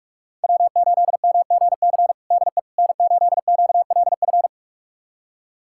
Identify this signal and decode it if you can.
Morse code, heard as a keyed tone, about 43 wpm, 700 Hz. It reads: W8MGC DE N8QLF
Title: USA Callsign DE Callsign - 5 Characters - 45wpm